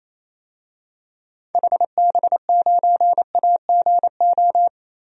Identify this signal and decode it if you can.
Morse code, heard as a keyed tone, about 28 wpm, 700 Hz. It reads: HB9AGO
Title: DXCC-100 Call Signs - 6 Characters - 28wpm